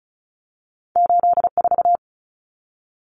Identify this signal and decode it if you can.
Morse code, heard as a keyed tone, about 36 wpm, 700 Hz. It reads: 84
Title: Common Cut Numbers - 2-Digit 35wpm